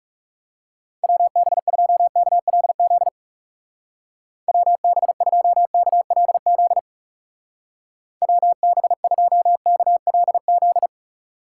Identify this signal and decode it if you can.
Morse code, heard as a keyed tone, about 36 wpm, 700 Hz. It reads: WB2KLZ WB2KLZ WB2KLZ
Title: Speed Racing - USA 2x3 Call Signs 30wpm